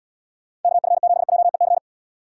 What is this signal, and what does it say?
Morse code, heard as a keyed tone, about 72 wpm, 700 Hz. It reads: KF7JEC